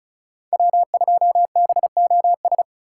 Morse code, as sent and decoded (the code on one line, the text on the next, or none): .-- ..--- -... --- ...
W2BOS